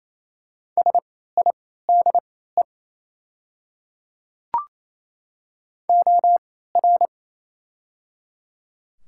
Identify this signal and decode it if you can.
Morse code, heard as a keyed tone, about 27 wpm, spaced as for 16 wpm, 700 Hz. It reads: SIDE OR
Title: 3x Character Spacing - Top 100 Words - 28wpm